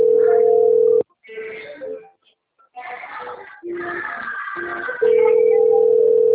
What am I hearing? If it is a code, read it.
Ringback tone (North America), 440 + 480 Hz.
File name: out- - - - - .704